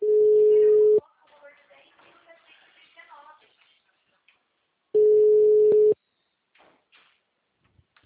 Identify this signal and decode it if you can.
Ringback tone (Europe, 425 Hz), with speech.